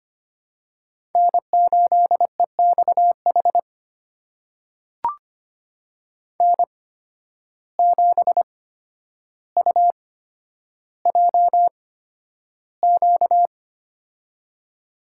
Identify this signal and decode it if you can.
Morse code, heard as a keyed tone, about 25 wpm, spaced as for 11 wpm, 700 Hz. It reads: N8EXH N7UJQ
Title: USA Call Signs - 5 Characters - 8x Character Spacing - 25wpm